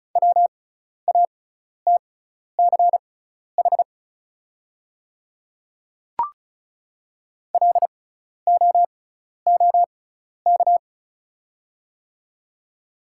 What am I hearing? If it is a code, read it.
Morse code, heard as a keyed tone, about 36 wpm, spaced as for 12 wpm, 690 Hz. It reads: WATCH LOOK
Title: 6x Character Spacing - Top 300 Words - 35wpm